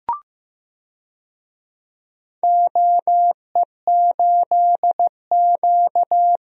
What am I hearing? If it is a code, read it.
Morse code, heard as a keyed tone, about 15 wpm, 700 Hz. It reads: OE8Q
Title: DXCC-100 Call Signs - 4 Characters - Repeated 2x - 15wpm